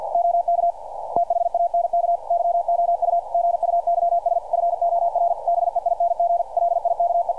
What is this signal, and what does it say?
Morse code, heard as a keyed tone, about 46 wpm, 690 Hz, with radio noise. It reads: OK THANK YOU GUYS FOR BEING LIKE
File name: archived / CRSnet